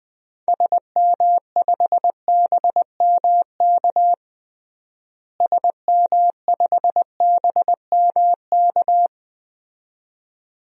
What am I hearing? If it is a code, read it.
Morse code, heard as a keyed tone, about 20 wpm, 700 Hz. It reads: SM5BMK SM5BMK
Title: DXCC-100 Call Signs - 6 Characters - Repeated 2x - 20wpm